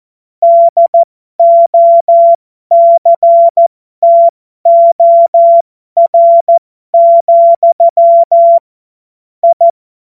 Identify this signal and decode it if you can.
Morse code, heard as a keyed tone, about 14 wpm, spaced as for 12 wpm, 680 Hz. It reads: DOCTOR, I
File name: Morsekurs / Witze